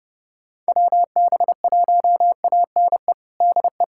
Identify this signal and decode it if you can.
Morse code, heard as a keyed tone, about 30 wpm, 700 Hz. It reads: WB1ANE DE
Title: USA Callsign DE Callsign - 6 Characters - 30wpm